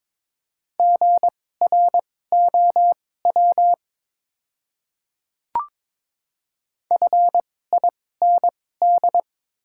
Morse code, heard as a keyed tone, about 22 wpm, spaced as for 16 wpm, 700 Hz. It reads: GROW FIND
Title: 2x Character Spacing - Top 500 Words - 22wpm